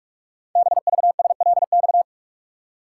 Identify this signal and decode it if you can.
Morse code, heard as a keyed tone, about 44 wpm, 700 Hz. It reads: BVSLX
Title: Code Groups - 5 Letter:Numbers - 45wpm